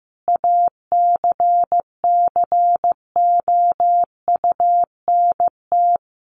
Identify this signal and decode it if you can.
Morse code, heard as a keyed tone, about 15 wpm, 700 Hz. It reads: ACCOUNT
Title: Sentences from Top 900 Words 15wpm